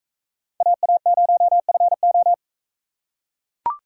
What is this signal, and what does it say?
Morse code, heard as a keyed tone, about 41 wpm, 700 Hz. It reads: AA0FO